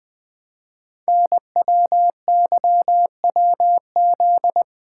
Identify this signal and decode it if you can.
Morse code, heard as a keyed tone, about 20 wpm, 700 Hz. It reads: NWYWZ
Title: Code Groups - 5 Letters - 20wpm